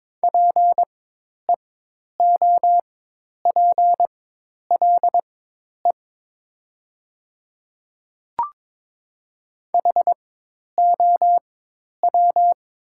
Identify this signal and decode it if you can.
Morse code, heard as a keyed tone, about 22 wpm, spaced as for 10 wpm, 700 Hz. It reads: PEOPLE HOW
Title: 4x Character Spacing - Top 100 Words - 22wpm